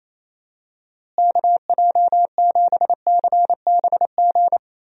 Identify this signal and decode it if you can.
Morse code, heard as a keyed tone, about 28 wpm, 700 Hz. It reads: KJ7CBG
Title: USA Call Signs - 6 Characters - Repeated 2x - 28wpm